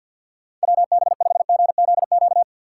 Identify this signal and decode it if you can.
Morse code, heard as a keyed tone, about 49 wpm, 700 Hz. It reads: WB5GZQ
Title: USA Call Signs - 6 Characters - 4x Character Spacing - 50wpm